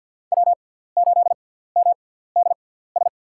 Morse code, heard as a keyed tone, about 50 wpm, spaced as for 17 wpm, 700 Hz. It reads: W8MDS
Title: USA Call Signs - 6x Character Spacing - 50wpm